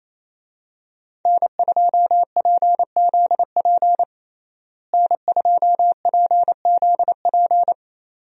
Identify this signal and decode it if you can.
Morse code, heard as a keyed tone, about 27 wpm, 700 Hz. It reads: N2PZP N2PZP
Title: USA Call Signs - 5 Characters - Repeated 2x - 28wpm